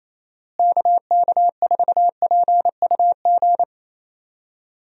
Morse code, heard as a keyed tone, about 28 wpm, 700 Hz. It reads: KK4PUG
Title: USA Call Signs - 6 Characters - Repeated 2x - 28wpm